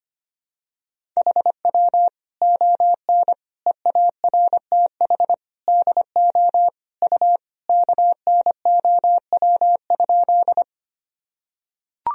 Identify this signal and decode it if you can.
Morse code, heard as a keyed tone, about 25 wpm, 700 Hz. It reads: HW ON EARTH DO U KNOW?
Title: Sentences from Top 200 Words and Abbreviations 25wpm